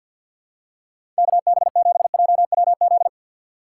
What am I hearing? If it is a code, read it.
Morse code, heard as a keyed tone, about 48 wpm, 700 Hz. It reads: KB7JWZ
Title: USA Call Signs - 6 Characters - 3x Character Spacing - 50wpm